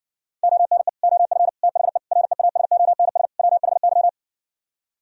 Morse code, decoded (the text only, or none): ONE OF THE REASONS WHY